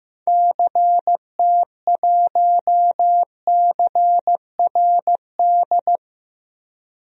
Morse code, heard as a keyed tone, about 15 wpm, 700 Hz. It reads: CT1CRD
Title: DXCC-100 Call Signs - 6 Characters - Repeated 2x - 15wpm